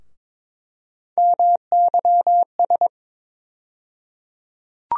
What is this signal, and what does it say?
Morse code, heard as a keyed tone, about 22 wpm, 700 Hz. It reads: MYS
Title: USA Call Sign Postfixes - 3 Letters 22wpm